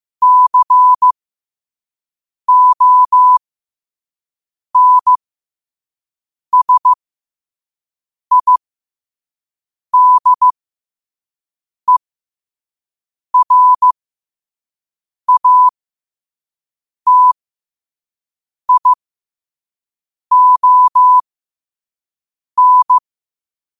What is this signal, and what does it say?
Morse code, heard as a keyed tone, about 15 wpm, spaced as for 5 wpm, 1000 Hz. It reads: CONSIDERATION
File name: qotdfiles / 5wpm